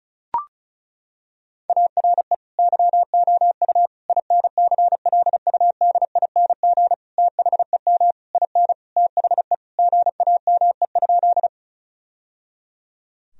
Morse code, heard as a keyed tone, about 36 wpm, 700 Hz. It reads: ARE YOU INCLUDING THEM IN THE GAME?
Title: Sentences from Top 1000 Words - Review Part 4 - 35wpm